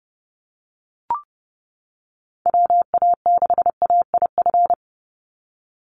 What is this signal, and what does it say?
Morse code, heard as a keyed tone, about 30 wpm, 700 Hz. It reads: WA6AIF
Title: USA Call Signs 30wpm